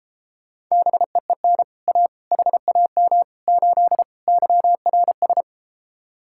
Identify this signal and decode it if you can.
Morse code, heard as a keyed tone, about 33 wpm, 700 Hz. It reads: BEEN A HAM 8 YRS